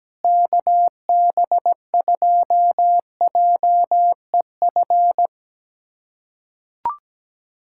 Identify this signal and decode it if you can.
Morse code, heard as a keyed tone, about 17 wpm, 700 Hz. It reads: KB2JEF
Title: USA Callsign DE Callsign - 6 Characters - 17wpm